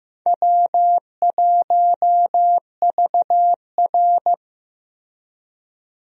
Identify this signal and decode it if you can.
Morse code, heard as a keyed tone, about 15 wpm, 700 Hz. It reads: W1VR
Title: USA Call Signs - 4 Characters - Repeated 2x - 15wpm